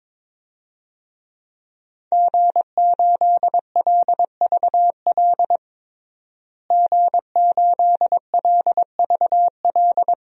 Morse code, heard as a keyed tone, about 22 wpm, 700 Hz. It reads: G8LVL G8LVL
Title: DXCC-100 Call Signs - 5 Characters - Repeated 3x - 22wpm